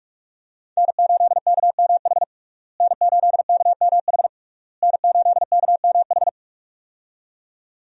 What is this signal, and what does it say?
Morse code, heard as a keyed tone, about 46 wpm, 700 Hz. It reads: N8KMH N8KMH N8KMH
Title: USA Call Signs - 5 Characters - Repeated 3x - 45wpm